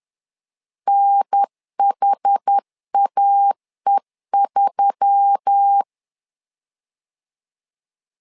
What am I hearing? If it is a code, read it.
Morse code, heard as a keyed tone, about 10 wpm, 800 Hz. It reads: NHAE3
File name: LEZIONI / 5 LBVC / 5 L 10wpm 2259